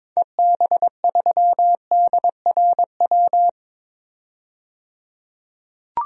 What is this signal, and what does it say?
Morse code, heard as a keyed tone, about 22 wpm, 700 Hz. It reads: EB3DRW
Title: DXCC-100 Call Signs - 6 Characters - 22wpm